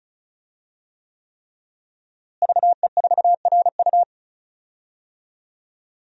Morse code, heard as a keyed tone, about 36 wpm, 700 Hz. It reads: VE4RU
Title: DXCC-100 Call Signs - 5 Characters - Repeated 3x - 35wpm